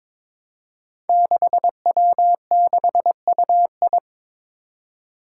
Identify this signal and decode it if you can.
Morse code, heard as a keyed tone, about 22 wpm, 700 Hz. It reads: 6W6UI
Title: Five Challenging Letter-Number Combinations 22wpm